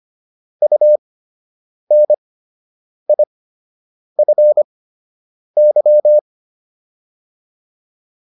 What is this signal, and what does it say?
Morse code, heard as a keyed tone, about 25 wpm, spaced as for 8 wpm, 600 Hz. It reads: UNIFY